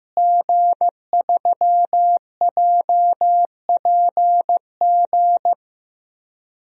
Morse code, heard as a keyed tone, about 16 wpm, 700 Hz. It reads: G3JPG
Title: DXCC-100 Call Signs - 5 Characters - Repeated 2x - 15wpm